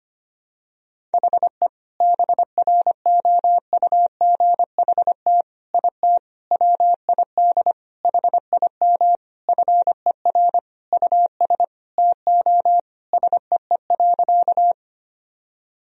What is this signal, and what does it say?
Morse code, heard as a keyed tone, about 25 wpm, 700 Hz. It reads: HE BROUGHT IT WID HIM FER US TO SEE.
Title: Sentences from Top 500 Words and Abbreviations 25wpm